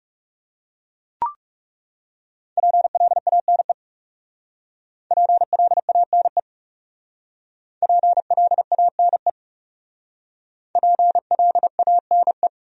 Morse code, heard as a keyed tone, about 38 wpm, 700 Hz. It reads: PLANE PLANE PLANE PLANE